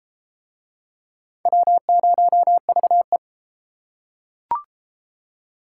Morse code, heard as a keyed tone, about 31 wpm, 700 Hz. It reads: W0VE